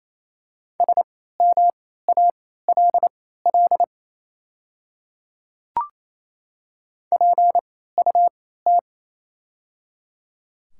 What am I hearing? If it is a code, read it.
Morse code, heard as a keyed tone, about 28 wpm, spaced as for 16 wpm, 700 Hz. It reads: SMALL PUT
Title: 3x Character Spacing - Top 200 Words - 28wpm